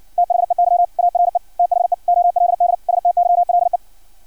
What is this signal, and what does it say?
Morse code, heard as a keyed tone, about 56 wpm, 700 Hz. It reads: THEY ARE THE OLD STYLE
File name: archived / CRSnet